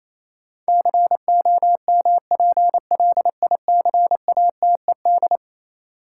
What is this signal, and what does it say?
Morse code, heard as a keyed tone, about 28 wpm, 700 Hz. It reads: COMPLICATED
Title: Sentences from Top 1200 Words - New Word Drill - 28wpm